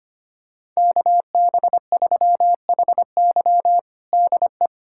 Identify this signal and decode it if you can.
Morse code, heard as a keyed tone, about 25 wpm, 700 Hz. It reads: KB3HY DE
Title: USA Callsign DE Callsign 25wpm